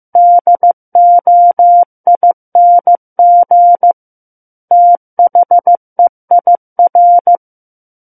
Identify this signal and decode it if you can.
Morse code, heard as a keyed tone, about 15 wpm, 700 Hz. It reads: DOING THEIR
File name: Book One / 06-Chapter 6 - 15 WPM